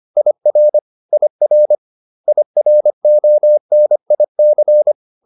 Morse code, heard as a keyed tone, about 25 wpm, 600 Hz. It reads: IR IR IRONIC
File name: IR6 25